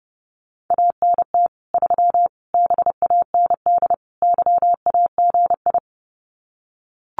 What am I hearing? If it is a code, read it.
Morse code, heard as a keyed tone, about 30 wpm, 700 Hz. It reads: ANT 3 BAND YAGI
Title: Common QSO Phrases 30wpm